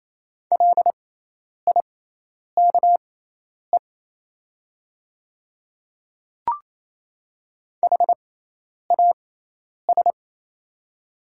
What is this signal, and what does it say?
Morse code, heard as a keyed tone, about 28 wpm, spaced as for 10 wpm, 700 Hz. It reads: LIKE HAS
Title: 6x Character Spacing - Top 100 Words - 28wpm